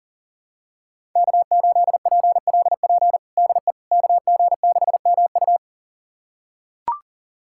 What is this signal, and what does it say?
Morse code, heard as a keyed tone, about 37 wpm, 700 Hz. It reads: K8PLP DE KG6MU
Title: USA Callsign DE Callsign - 5 Characters - 40wpm